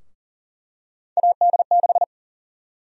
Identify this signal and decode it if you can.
Morse code, heard as a keyed tone, about 37 wpm, 700 Hz. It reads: AD6
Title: USA Call Sign Prefixes - Repeated 2x - 40wpm